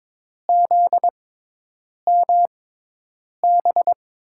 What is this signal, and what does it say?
Morse code, heard as a keyed tone, about 22 wpm, spaced as for 8 wpm, 700 Hz. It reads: ZMB
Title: USA Call Signs - 6 Characters - 6x Character Spacing - 22wpm